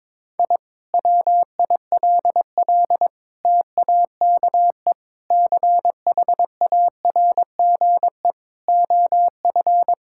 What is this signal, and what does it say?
Morse code, heard as a keyed tone, about 22 wpm, 700 Hz. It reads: I WILL TAKE CHARGE OF
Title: Sentences from Top 1000 Words - Review Part 6 - 22wpm